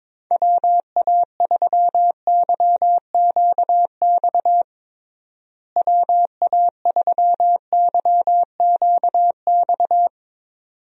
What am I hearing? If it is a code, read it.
Morse code, heard as a keyed tone, about 22 wpm, 700 Hz. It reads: WA3YQX WA3YQX
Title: USA Call Signs - 6 Characters - Repeated 2x - 22wpm